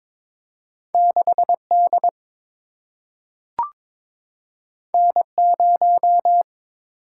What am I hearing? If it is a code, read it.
Morse code, heard as a keyed tone, about 22 wpm, 700 Hz. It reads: 6 D N 0
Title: Single-Letter Single-Number 22wpm